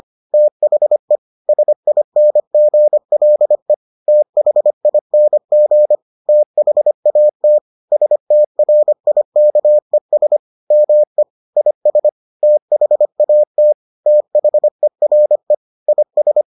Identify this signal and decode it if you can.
Morse code, heard as a keyed tone, about 25 wpm, 600 Hz. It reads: THE SINGLE THING THAT STRIKES ME IS THAT THERE IS